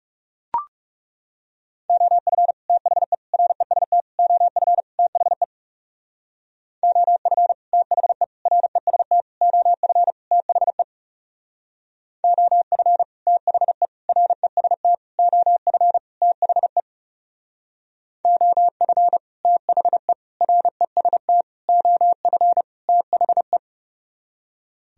Morse code, heard as a keyed tone, about 39 wpm, 700 Hz. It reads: OF THE REST OF THE OF THE REST OF THE OF THE REST OF THE OF THE REST OF THE